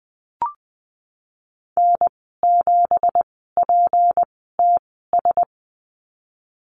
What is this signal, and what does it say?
Morse code, heard as a keyed tone, about 20 wpm, spaced as for 15 wpm, 700 Hz. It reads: N7PTS
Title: 186 Lesson 30g - Letter b - Call Signs 20wpm